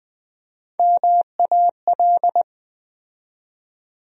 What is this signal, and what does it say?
Morse code, heard as a keyed tone, about 20 wpm, 700 Hz. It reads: MAL
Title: USA Call Sign Postfixes - 3 Letters 20wpm